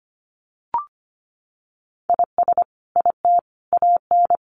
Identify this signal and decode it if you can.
Morse code, heard as a keyed tone, about 25 wpm, 700 Hz. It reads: IS IT AN
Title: Sentences from Top 1400 Words 25wpm